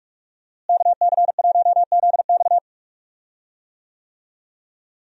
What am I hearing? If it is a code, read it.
Morse code, heard as a keyed tone, about 41 wpm, 700 Hz. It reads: KC1ZX
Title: USA Call Signs - 5 Characters - Repeated 2x - 45wpm